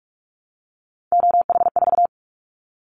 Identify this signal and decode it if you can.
Morse code, heard as a keyed tone, about 44 wpm, 700 Hz. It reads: OH4
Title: Callsign Prefixes for Top 25 DXCC - Repeated 3x - 45wpm